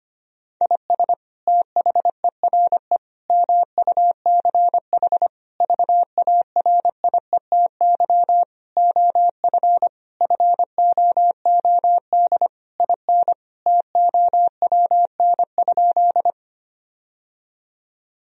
Morse code, heard as a keyed tone, about 25 wpm, 700 Hz. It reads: IS THERE MUCH VARIETY OF FOOD IN TOWN?
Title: Sentences from Top 1000 Words - Review Part 4 - 25wpm